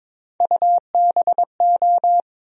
Morse code, heard as a keyed tone, about 22 wpm, 700 Hz. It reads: UBO
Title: USA Call Sign Postfixes - 3 Letters 22wpm